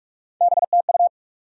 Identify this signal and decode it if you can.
Morse code, heard as a keyed tone, about 46 wpm, 700 Hz. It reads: BTU
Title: Top 90 QSO Elements 45wpm